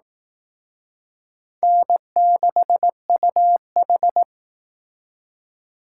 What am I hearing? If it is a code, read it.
Morse code, heard as a keyed tone, about 18 wpm, 700 Hz. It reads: N6UH
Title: Speed Racing - USA 1x2 Call Signs 15wpm